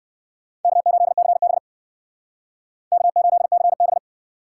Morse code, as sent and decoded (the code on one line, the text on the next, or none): -.- ---.. -.-. -... / -.- ---.. -.-. -...
K8CB K8CB